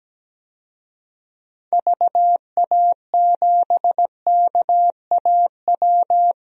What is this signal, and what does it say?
Morse code, heard as a keyed tone, about 17 wpm, 700 Hz. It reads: VA7KAW
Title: DXCC-100 Call Signs - 6 Characters - Repeated 2x - 17wpm